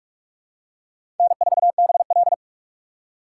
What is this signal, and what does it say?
Morse code, heard as a keyed tone, about 43 wpm, 700 Hz. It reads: N4BL
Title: USA Call Signs - 4 Characters - 6x Character Spacing - 45wpm